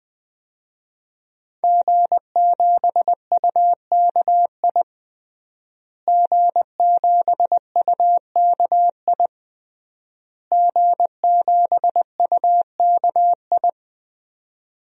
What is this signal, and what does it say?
Morse code, heard as a keyed tone, about 20 wpm, 700 Hz. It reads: G7UKI G7UKI G7UKI
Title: DXCC-100 Call Signs - 5 Characters - Repeated 3x - 20wpm